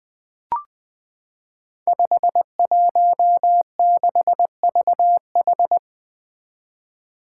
Morse code, heard as a keyed tone, about 20 wpm, 700 Hz. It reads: 516VH
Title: Five Challenging Letter-Number Combinations 20wpm